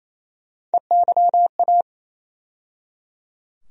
Morse code, heard as a keyed tone, about 27 wpm, 700 Hz. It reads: EYA
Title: USA Call Sign Postfixes - 3 Letters 28wpm